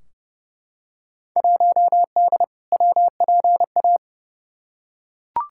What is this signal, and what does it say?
Morse code, heard as a keyed tone, about 30 wpm, 700 Hz. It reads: 1D WPA
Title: Field Day Exchanges - 30wpm